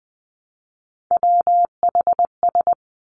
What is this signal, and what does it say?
Morse code, heard as a keyed tone, about 20 wpm, 700 Hz. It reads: WHS
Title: Three Challenging Letter-Number Combinations 20wpm